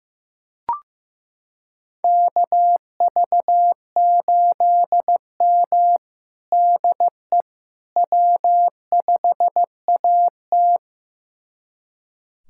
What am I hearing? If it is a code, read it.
Morse code, heard as a keyed tone, about 15 wpm, 700 Hz. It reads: KV8M DE W5AT
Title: USA Callsign DE Callsign - 4 Characters - 15wpm